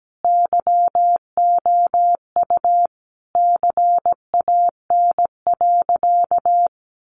Morse code, heard as a keyed tone, about 17 wpm, 700 Hz. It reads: YOU CAN.
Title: Sentences from Top 600 Words 17wpm